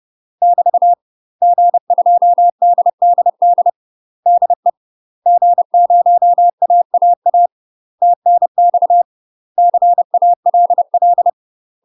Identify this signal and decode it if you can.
Morse code, heard as a keyed tone, about 30 wpm, 700 Hz. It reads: <BT> G2DDD DE G0AAA TNX CALL
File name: UK MASTER QSOs TRAINING / 30WPM / UK MASTER QSOs 0027